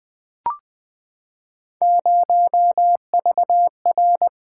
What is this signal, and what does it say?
Morse code, heard as a keyed tone, about 20 wpm, 700 Hz. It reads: 0VR
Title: Three Letter-Number Combinations - 20wpm